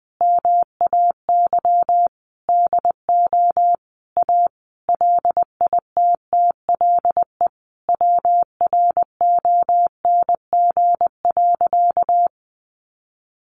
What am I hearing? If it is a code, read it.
Morse code, heard as a keyed tone, about 20 wpm, 700 Hz. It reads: MAY DO A LITTLE WRONG.
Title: Sentences from Top 900 Words 20wpm